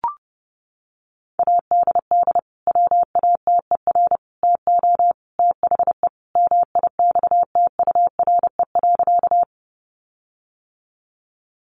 Morse code, heard as a keyed tone, about 30 wpm, 700 Hz. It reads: ADD WATER TO THE MIXTURE.
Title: Sentences from Top 1500 Words 30wpm